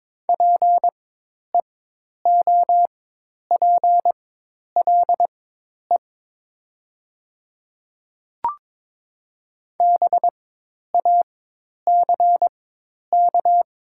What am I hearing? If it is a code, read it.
Morse code, heard as a keyed tone, about 22 wpm, spaced as for 10 wpm, 700 Hz. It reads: PEOPLE BACK
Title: 4x Character Spacing - Top 200 Words - 22wpm